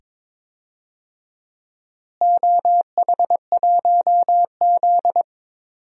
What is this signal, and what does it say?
Morse code, heard as a keyed tone, about 22 wpm, 700 Hz. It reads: OH1Z
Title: DXCC-100 Call Signs - 4 Characters - Repeated 2x - 22wpm